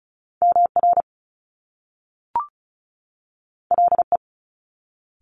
Morse code, heard as a keyed tone, about 36 wpm, 700 Hz. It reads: MR LE
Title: Top Two Letter Combinations in Words 35wpm